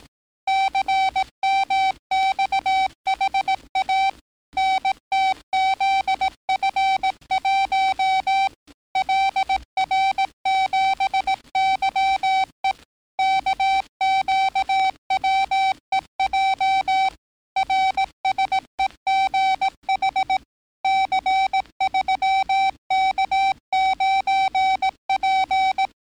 Morse code, heard as a keyed tone, about 18 wpm, 770 Hz. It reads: CMXHA NTZF1 LR7YE KQWEJ RSEGH C3K9P
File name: training / NAVPERS 11049to11060 / NAVPERS 11058 RA RB